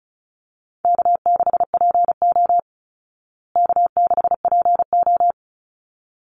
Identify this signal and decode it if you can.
Morse code, heard as a keyed tone, about 34 wpm, 700 Hz. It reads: K6PO K6PO
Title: Call Signs Repeated 3x - 35wpm